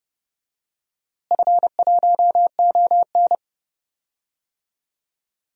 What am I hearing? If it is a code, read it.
Morse code, heard as a keyed tone, about 29 wpm, 700 Hz. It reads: F1ON
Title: DXCC-100 Call Signs - 4 Characters - Repeated 3x - 30wpm